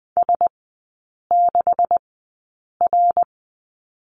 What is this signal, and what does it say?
Morse code, heard as a keyed tone, about 20 wpm, 700 Hz. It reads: S 6 R
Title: 127 Lesson 21e - Number 6 - Three Characters 20wpm